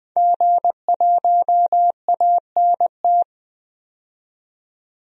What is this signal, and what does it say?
Morse code, heard as a keyed tone, about 20 wpm, 700 Hz. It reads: G1ANT
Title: DXCC-100 Call Signs - 5 Characters - 20wpm